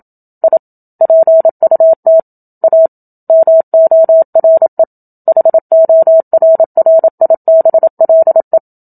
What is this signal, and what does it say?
Morse code, heard as a keyed tone, about 27 wpm, 650 Hz. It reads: I PUT A MORE HORRIBLE
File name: Book Two / 01-Chapter 1 - 27 WPM